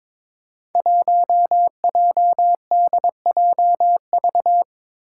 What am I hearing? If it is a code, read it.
Morse code, heard as a keyed tone, about 21 wpm, 700 Hz. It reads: 1JDJV
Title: Five Challenging Letter-Number Combinations 22wpm